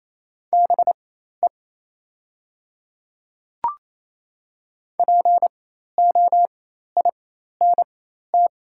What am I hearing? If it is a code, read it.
Morse code, heard as a keyed tone, about 28 wpm, spaced as for 13 wpm, 700 Hz. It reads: BE POINT
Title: 4x Character Spacing - Top 200 Words - 28wpm